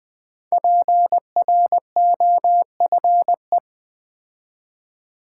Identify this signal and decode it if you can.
Morse code, heard as a keyed tone, about 20 wpm, 700 Hz. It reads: PROFE
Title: Build A Word - 5 Letters At A Time - 12-Letter Words - 20wpm